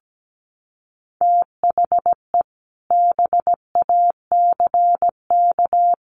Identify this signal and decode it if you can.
Morse code, heard as a keyed tone, about 17 wpm, 700 Hz. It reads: THE BACK
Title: Sets of 5 Words Encore 17wpm